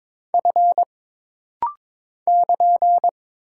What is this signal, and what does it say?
Morse code, heard as a keyed tone, about 22 wpm, 700 Hz. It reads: F <KN>
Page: This rapid-fire timing is geared to push your ICR (Instant Character Recognition) skills.